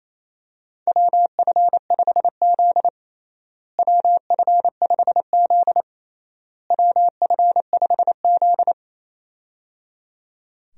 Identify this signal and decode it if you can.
Morse code, heard as a keyed tone, about 27 wpm, 700 Hz. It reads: WF5Z WF5Z WF5Z
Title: USA Call Signs - 4 Characters - Repeated 3x - 28wpm